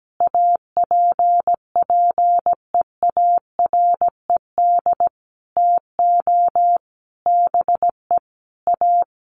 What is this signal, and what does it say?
Morse code, heard as a keyed tone, about 17 wpm, 700 Hz. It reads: APPEARED TO BE A
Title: Sets of 5 Words Encore2 17wpm